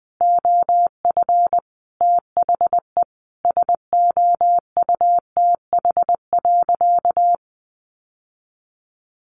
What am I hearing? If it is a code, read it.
Morse code, heard as a keyed tone, about 20 wpm, 700 Hz. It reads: OF THE SOUTH.
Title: Sentences from Top 1500 Words 20wpm